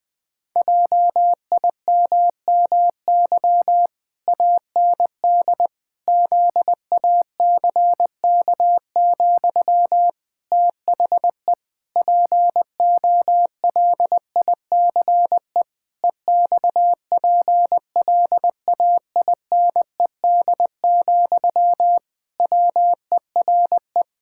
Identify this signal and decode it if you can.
Morse code, heard as a keyed tone, about 20 wpm, 700 Hz. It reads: JIMMY AND ZACK, THE POLICE EXPLAINED, WERE
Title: Pangram Challenge v2 - 20wpm